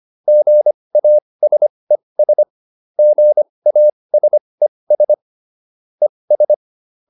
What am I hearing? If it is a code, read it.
Morse code, heard as a keyed tone, about 25 wpm, spaced as for 20 wpm, 600 Hz. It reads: GASES GASES ES